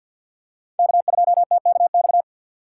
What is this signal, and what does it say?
Morse code, heard as a keyed tone, about 53 wpm, 700 Hz. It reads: K2TKX
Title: USA Call Signs - 5 Characters - 8x Character Spacing - 50wpm